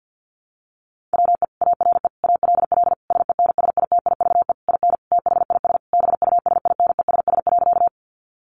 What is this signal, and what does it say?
Morse code, heard as a keyed tone, about 50 wpm, 700 Hz. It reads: WE ARE ALL SENSITIVE IN THIS BUSINESS.
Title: Sentences from Top 1400 Words 50wpm